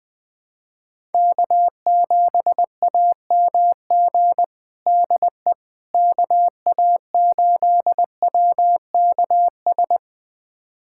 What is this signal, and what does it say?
Morse code, heard as a keyed tone, about 20 wpm, 700 Hz. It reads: K7AMG DE KA8WKS
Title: USA Callsign DE Callsign 20wpm